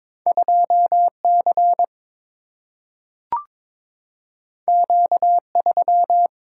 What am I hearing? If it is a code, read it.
Morse code, heard as a keyed tone, about 22 wpm, 700 Hz. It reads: 2 C Q 3
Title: Single-Letter Single-Number 22wpm